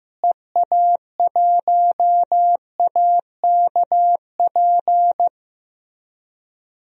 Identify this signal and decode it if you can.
Morse code, heard as a keyed tone, about 15 wpm, 700 Hz. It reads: EA1AKP
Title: DXCC-100 Call Signs - 6 Characters - Repeated 3x - 15wpm